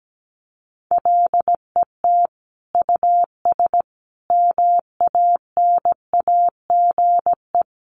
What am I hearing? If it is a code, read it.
Morse code, heard as a keyed tone, about 17 wpm, 700 Hz. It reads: LET US MANAGE
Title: Sentences from Top 1400 Words 17wpm